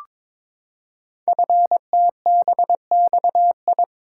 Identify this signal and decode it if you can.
Morse code, heard as a keyed tone, about 22 wpm, 700 Hz. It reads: FTBXI
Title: Code Groups - 5 Letters - 22wpm